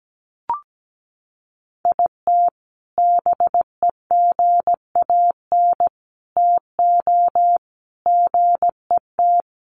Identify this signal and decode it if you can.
Morse code, heard as a keyed tone, about 17 wpm, 700 Hz. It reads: IT BEGAN TO GET
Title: Sentences from Top 1400 Words 17wpm